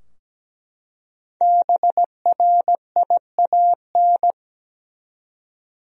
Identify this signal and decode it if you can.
Morse code, heard as a keyed tone, about 17 wpm, 700 Hz. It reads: BRIAN
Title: USA Common Names - 17wpm